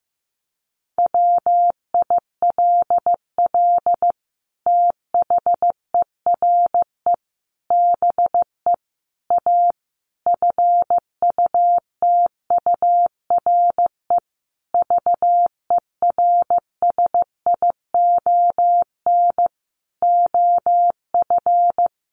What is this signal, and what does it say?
Morse code, heard as a keyed tone, about 15 wpm, 700 Hz. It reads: WILL THERE BE A FUTURE VERSION OF
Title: Sentences from Top 1400 Words 15wpm